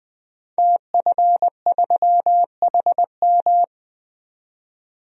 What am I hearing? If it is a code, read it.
Morse code, heard as a keyed tone, about 20 wpm, 700 Hz. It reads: TF3HM
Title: DXCC-100 Call Signs - 5 Characters - 20wpm